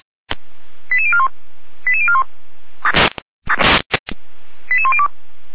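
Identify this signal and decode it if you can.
Five-tone selective call (ZVEI), heard as quick runs of five tones, tones of about 70 ms. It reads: 88041 88041 88102
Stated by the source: ZVEI five-tone selective call received by SDR, from the feuerwehr-orlamuende.org fire brigade server, address 88124.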